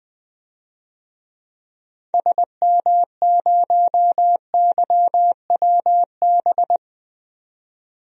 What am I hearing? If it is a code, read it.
Morse code, heard as a keyed tone, about 20 wpm, 700 Hz. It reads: SM0YWB
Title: DXCC-100 Call Signs - 6 Characters - 20wpm